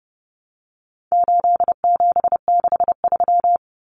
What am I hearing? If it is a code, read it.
Morse code, heard as a keyed tone, about 30 wpm, 700 Hz. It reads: 8763
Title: Common Cut Numbers - 4-Digit 30wpm